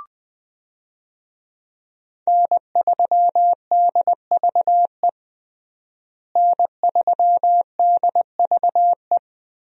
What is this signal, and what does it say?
Morse code, heard as a keyed tone, about 21 wpm, 700 Hz. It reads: N3DVE N3DVE
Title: USA Call Signs - 5 Characters - Repeated 3x - 20wpm